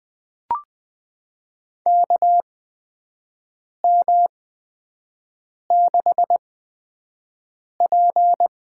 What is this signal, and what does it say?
Morse code, heard as a keyed tone, about 20 wpm, spaced as for 5 wpm, 700 Hz. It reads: KM6P
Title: USA Call Signs - 6 Characters - 8x Character Spacing - 20wpm